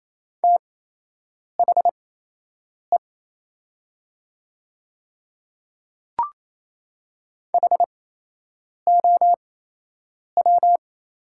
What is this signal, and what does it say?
Morse code, heard as a keyed tone, about 28 wpm, spaced as for 8 wpm, 700 Hz. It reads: THE HOW
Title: 8x Character Spacing - Top 100 Words - 28wpm